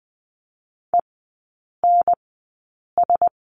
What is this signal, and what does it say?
Morse code, heard as a keyed tone, about 20 wpm, spaced as for 8 wpm, 700 Hz. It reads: ENS
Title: 074 Lesson 13e - Letter L - Three Characters 20wpm